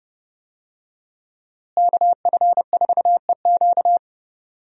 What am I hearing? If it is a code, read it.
Morse code, heard as a keyed tone, about 32 wpm, 700 Hz. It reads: KF4EQ
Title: USA Call Signs - 5 Characters - Repeated 2x - 30wpm